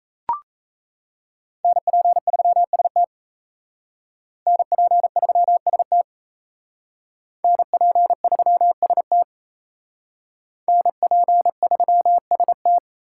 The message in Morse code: NP3ST NP3ST NP3ST NP3ST

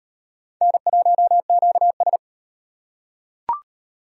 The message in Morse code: N1QS

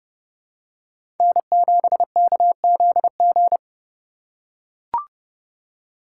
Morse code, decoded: N7KZG

